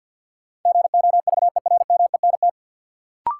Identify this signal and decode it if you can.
Morse code, heard as a keyed tone, about 52 wpm, 700 Hz. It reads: GOVERMENT
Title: Sentences from Top 1300 Words - New Word Drill - 50wpm